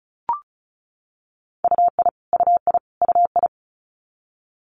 Morse code, heard as a keyed tone, about 36 wpm, 700 Hz. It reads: UI UI UI
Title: Multi-letter Phonemes Repeated 3x 35wpm